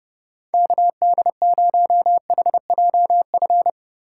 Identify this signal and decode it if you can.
Morse code, heard as a keyed tone, about 36 wpm, 700 Hz. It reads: KD0HJF
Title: Speed Racing - USA 2x3 Call Signs 30wpm